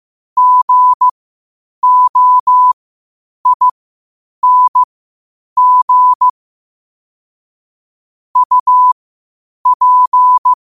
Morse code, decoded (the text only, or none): GOING UP